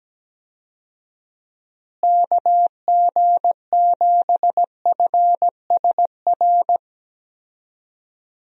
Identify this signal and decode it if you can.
Morse code, heard as a keyed tone, about 17 wpm, 700 Hz. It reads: KG7FSR
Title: USA Call Signs - 6 Characters - Repeated 2x - 17wpm